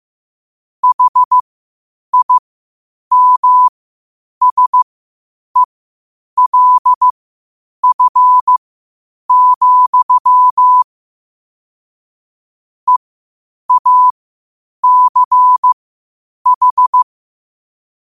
Morse code, heard as a keyed tone, about 15 wpm, spaced as for 8 wpm, 1000 Hz. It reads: HIMSELF, EACH
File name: qotdfiles / 8wpm